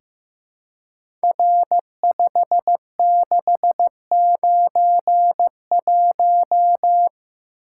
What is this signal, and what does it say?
Morse code, heard as a keyed tone, about 15 wpm, 700 Hz. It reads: R5691
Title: Code Groups - 5 Letter:Numbers - 15wpm